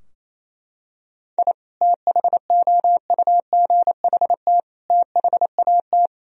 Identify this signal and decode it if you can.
Morse code, heard as a keyed tone, about 28 wpm, 700 Hz. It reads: I THOUGHT THAT
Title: Sentences from Top 1000 Words - Review Part 3 - 28wpm